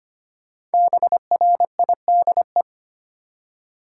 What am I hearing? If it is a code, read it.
Morse code, heard as a keyed tone, about 25 wpm, 700 Hz. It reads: BRIDE
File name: 5 Letter Words - First 3 Letters Given - 25wpm